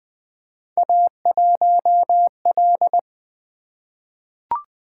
Morse code, heard as a keyed tone, about 20 wpm, 700 Hz. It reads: A1L
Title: Three Letter-Number Combinations - 20wpm